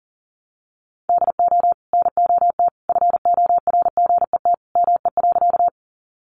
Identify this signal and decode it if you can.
Morse code, heard as a keyed tone, about 40 wpm, 700 Hz. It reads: DO NOT FORGET ME.
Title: Sentences from Top 1200 Words 40wpm